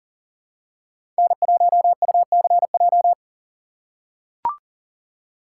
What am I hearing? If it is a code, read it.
Morse code, heard as a keyed tone, about 40 wpm, 700 Hz. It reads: N1UCJ
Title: USA Call Signs - 5 Characters - 4x Character Spacing - 40wpm